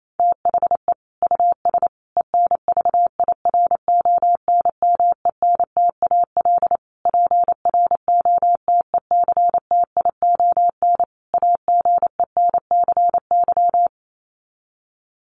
Morse code, heard as a keyed tone, about 28 wpm, 700 Hz. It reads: THE US ENVIRONMENTAL PROTECTION AGENCY
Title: Sets of 5 Words Encore2 28wpm